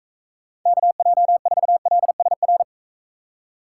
Morse code, heard as a keyed tone, about 44 wpm, 700 Hz. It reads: KJ4LSR